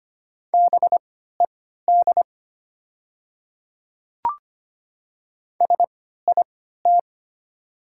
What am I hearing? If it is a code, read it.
Morse code, heard as a keyed tone, about 25 wpm, spaced as for 14 wpm, 690 Hz. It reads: BED SIT
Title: 3x Character Spacing - Top 500 Words - 25wpm